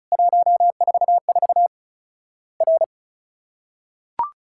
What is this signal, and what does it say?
Morse code, heard as a keyed tone, about 36 wpm, 700 Hz. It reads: 144 R
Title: Worked All Europe DX Contest - QTC Exchange - 35wpm